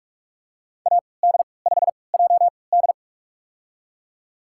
Morse code, heard as a keyed tone, about 43 wpm, spaced as for 25 wpm, 700 Hz. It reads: AD5JD
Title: USA Call Signs - 5 Characters - 3x Character Spacing - 45wpm